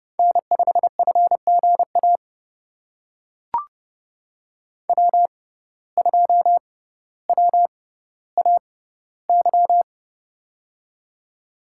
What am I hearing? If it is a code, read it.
Morse code, heard as a keyed tone, about 32 wpm, spaced as for 16 wpm, 700 Hz. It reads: N5FGA W2WAY
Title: USA Call Signs - 5 Characters - 6x Character Spacing - 30wpm